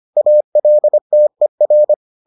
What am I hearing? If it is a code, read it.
Morse code, heard as a keyed tone, about 25 wpm, 600 Hz. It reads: ALTER